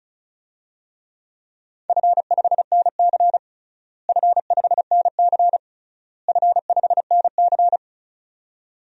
Morse code, heard as a keyed tone, about 34 wpm, 700 Hz. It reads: F5NC F5NC F5NC
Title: DXCC-100 Call Signs - 4 Characters - Repeated 3x - 35wpm